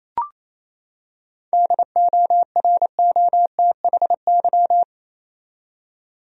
Morse code, heard as a keyed tone, about 27 wpm, 700 Hz. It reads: DOROTHY
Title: USA Common Names - 28wpm